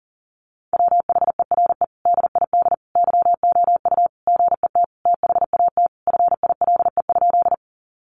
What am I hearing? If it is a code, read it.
Morse code, heard as a keyed tone, about 40 wpm, 700 Hz. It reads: WHERE DID YOU GET THAT FILE?
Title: Sentences from Top 1200 Words 40wpm